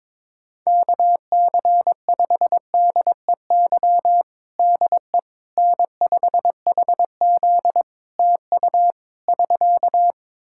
Morse code, heard as a keyed tone, about 22 wpm, 700 Hz. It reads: KC5DEY DE N5HZ TU <SK>
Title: Practice QSO - Exchange 4 - Ending - 22wpm